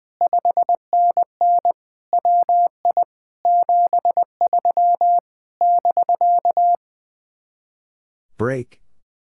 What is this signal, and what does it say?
Morse code, heard as a keyed tone, about 20 wpm, 700 Hz. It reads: 5NN WI 73 <BK>
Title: Speed-Racing Common QSO Phrases - 20wpm